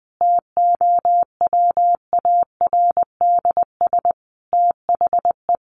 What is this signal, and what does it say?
Morse code, heard as a keyed tone, about 20 wpm, 700 Hz. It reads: TOWARDS THE
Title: Sentences from Top 1100 Words 20wpm